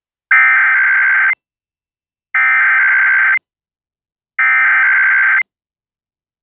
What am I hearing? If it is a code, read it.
Emergency Alert System (SAME) tones (header bursts), heard as harsh buzzing bursts. It is heard from 0.3 to 5.4 s.